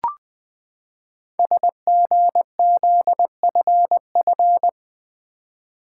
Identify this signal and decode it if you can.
Morse code, heard as a keyed tone, about 20 wpm, 700 Hz. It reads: SGZFF
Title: Code Groups - 5 Letters - 20wpm